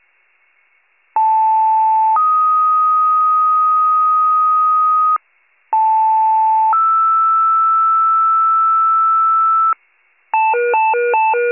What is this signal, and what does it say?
Two-tone page, heard as a one-second tone, then a three-second tone, about 880 then 1250 Hz, about 880 then 1320 Hz. It is heard from 1.2 to 9.7 s.